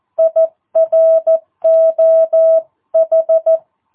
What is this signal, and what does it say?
Morse code, heard as a keyed tone, about 14 wpm, 650 Hz. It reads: IROH
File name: jh8cbh / 2022cw